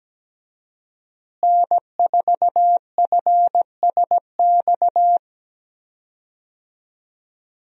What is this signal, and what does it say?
Morse code, heard as a keyed tone, about 17 wpm, 700 Hz. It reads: N4FSX
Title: USA Call Signs - 5 Characters - Repeated 3x - 17wpm